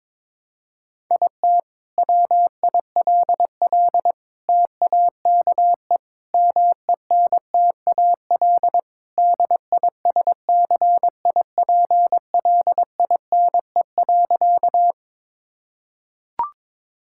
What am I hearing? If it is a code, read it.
Morse code, heard as a keyed tone, about 22 wpm, 700 Hz. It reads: IT WILL TAKE MENTAL DISCIPLINE.
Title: Sentences from Top 1000 Words - Review Part 6 - 22wpm